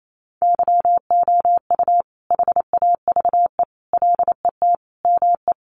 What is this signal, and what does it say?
Morse code, heard as a keyed tone, about 28 wpm, 700 Hz. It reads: YOU HAVE LET ME
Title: Sentences from Top 1500 Words 28wpm